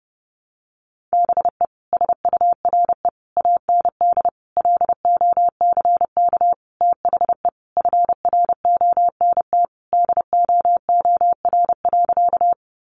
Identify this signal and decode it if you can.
Morse code, heard as a keyed tone, about 30 wpm, 700 Hz. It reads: BE SURE AND LOCK THE FRONT DOOR.
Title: Sentences from Top 1300 Words 30wpm